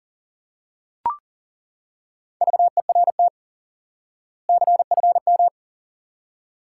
Morse code, heard as a keyed tone, about 40 wpm, 700 Hz. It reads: VERT CFM
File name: IQE Course 3 - Lesson 89 - YAGI - Two Elements - 40wpm